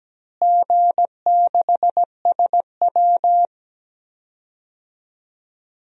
Morse code, heard as a keyed tone, about 17 wpm, 700 Hz. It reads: G6SW
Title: DXCC-100 Call Signs - 4 Characters - Repeated 2x - 17wpm